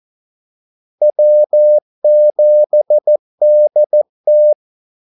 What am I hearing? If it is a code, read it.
Morse code, heard as a keyed tone, about 14 wpm, 600 Hz. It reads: W7DT
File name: Callsigns-14wpm